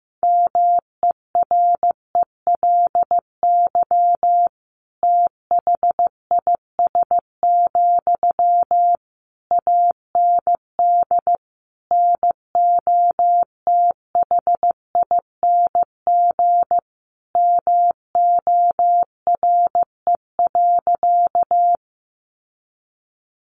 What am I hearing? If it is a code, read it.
Morse code, heard as a keyed tone, about 15 wpm, 700 Hz. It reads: MERELY THIS, AND NOTHING MORE.
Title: Sentences from Top 1400 Words 15wpm